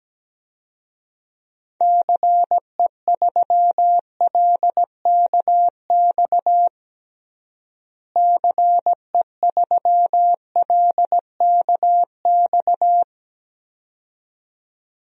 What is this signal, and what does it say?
Morse code, heard as a keyed tone, about 17 wpm, 700 Hz. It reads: CE3LKX CE3LKX
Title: DXCC-100 Call Signs - 6 Characters - Repeated 2x - 17wpm